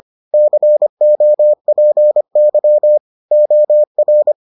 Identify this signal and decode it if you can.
Morse code, heard as a keyed tone, about 25 wpm, 600 Hz. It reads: COPY OR